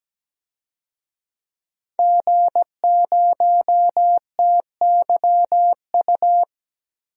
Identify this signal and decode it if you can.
Morse code, heard as a keyed tone, about 17 wpm, 700 Hz. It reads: G0TYU
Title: DXCC-100 Call Signs - 5 Characters - Repeated 2x - 17wpm